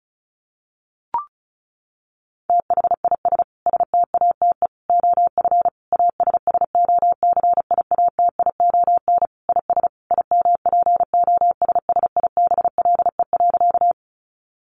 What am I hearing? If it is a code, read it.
Morse code, heard as a keyed tone, about 34 wpm, 700 Hz. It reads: THIS STATE OF ASSOCIATION IS IMPOSSIBLE.
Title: Sentences from Top 1100 Words 35wpm